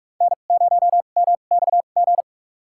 Morse code, heard as a keyed tone, about 45 wpm, spaced as for 32 wpm, 690 Hz. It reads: N0MXG
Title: USA Call Signs - 5 Characters - 2x Character Spacing - 45wpm